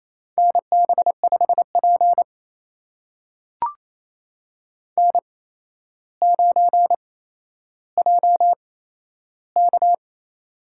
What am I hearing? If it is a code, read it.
Morse code, heard as a keyed tone, about 27 wpm, spaced as for 12 wpm, 700 Hz. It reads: NB5P N9JK
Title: USA Call Signs - 8x Character Spacing - 28wpm